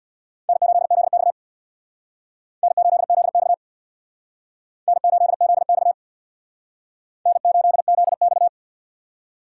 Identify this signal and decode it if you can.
Morse code, heard as a keyed tone, about 62 wpm, 700 Hz. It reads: N8ZX N8ZX N8ZX N8ZX
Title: Speed Racing - USA 1x2 Call Signs 50wpm